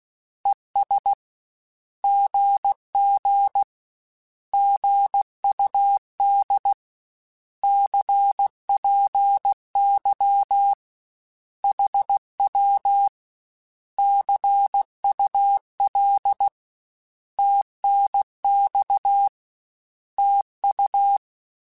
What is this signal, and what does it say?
Morse code, heard as a keyed tone, about 16 wpm, 800 Hz. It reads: ES GG GUD CPY HW CUL TNX TU
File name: Short Runs / Short Runs-mp3s / Words @15